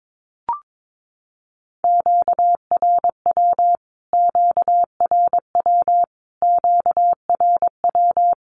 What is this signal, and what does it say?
Morse code, heard as a keyed tone, about 22 wpm, 700 Hz. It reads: QRW QRW QRW
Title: Q-Codes Repeated 3x 22wpm